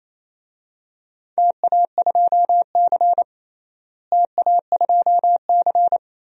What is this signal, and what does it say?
Morse code, heard as a keyed tone, about 28 wpm, 700 Hz. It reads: TA2C TA2C
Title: DXCC-100 Call Signs - 4 Characters - Repeated 3x - 28wpm